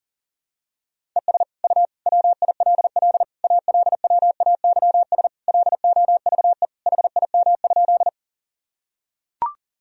Morse code, heard as a keyed tone, about 37 wpm, 700 Hz. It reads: ES U WILL ALWAYS LOVE HIM?
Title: Sentences from Top 400 Words and Abbreviations 40wpm